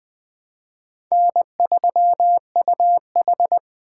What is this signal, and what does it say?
Morse code, heard as a keyed tone, about 23 wpm, 700 Hz. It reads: N3UH
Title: Speed Racing - USA 1x2 Call Signs 20wpm